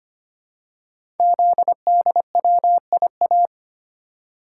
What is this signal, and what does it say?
Morse code, heard as a keyed tone, about 25 wpm, 700 Hz. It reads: ZDWIA
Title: Code Groups - 5 Letters - 25wpm